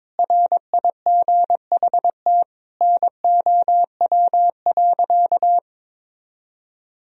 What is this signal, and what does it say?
Morse code, heard as a keyed tone, about 22 wpm, 700 Hz. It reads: RIGHT NOW.
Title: Sentences from Top 1000 Words - Review Part 4 - 22wpm